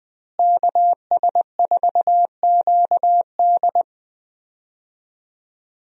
Morse code, heard as a keyed tone, about 20 wpm, 700 Hz. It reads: KS4QD
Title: USA Call Signs - 5 Characters - Repeated 2x - 20wpm